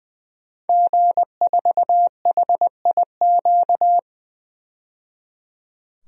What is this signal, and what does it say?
Morse code, heard as a keyed tone, about 20 wpm, 700 Hz. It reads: G4HIQ
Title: DXCC-100 Call Signs - 5 Characters - 20wpm